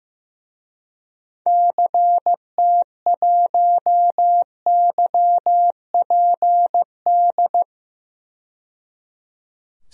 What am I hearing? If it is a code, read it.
Morse code, heard as a keyed tone, about 15 wpm, 700 Hz. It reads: CT1YPD
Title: DXCC-100 Call Signs - 6 Characters - 15wpm